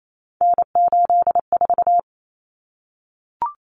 Morse code, heard as a keyed tone, about 28 wpm, 700 Hz. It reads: N84
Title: Common Cut Numbers - 3-Digit 28wpm